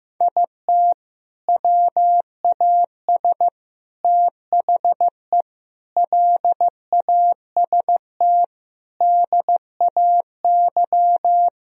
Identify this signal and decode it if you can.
Morse code, heard as a keyed tone, about 15 wpm, 700 Hz. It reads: IT WAS THE LAST DAY
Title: Sentences from Top 1000 Words - Review Part 1 - 15wpm